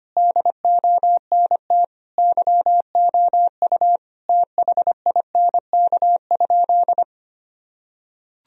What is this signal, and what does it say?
Morse code, heard as a keyed tone, about 25 wpm, 700 Hz. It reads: DONT YOU THINK?
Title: Sentences from Top 1000 Words - Review Part 4 - 25wpm